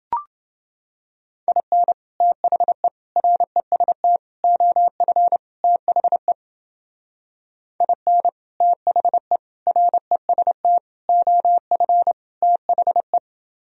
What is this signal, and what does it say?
Morse code, heard as a keyed tone, about 28 wpm, 700 Hz. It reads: IN THE REST OF THE IN THE REST OF THE